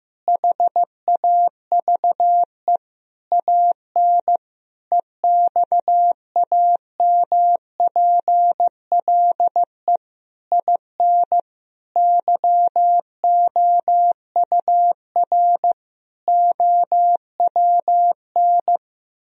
Morse code, heard as a keyed tone, about 15 wpm, 700 Hz. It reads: HAVE AN EXAMPLE IN YOUR OWN
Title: Sentences from Top 1000 Words - Review Part 1 - 15wpm